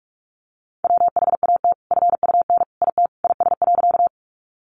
Morse code, heard as a keyed tone, about 44 wpm, 700 Hz. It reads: WHAT FUN IT IS.
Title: Sentences from Top 700 Words 45wpm